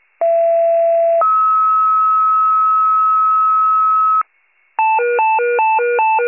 Two-tone page, heard as a one-second tone, then a three-second tone, about 670 then 1290 Hz. It is heard from 0.2 to 4.2 s.